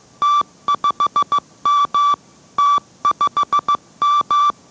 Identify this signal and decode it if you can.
Morse code, heard as a keyed tone, about 15 wpm, 1160 Hz. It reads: T5M T5M